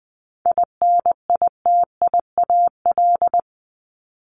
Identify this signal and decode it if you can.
Morse code, heard as a keyed tone, about 20 wpm, 700 Hz. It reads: INITIAL
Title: Top 1501-1750 Words 20wpm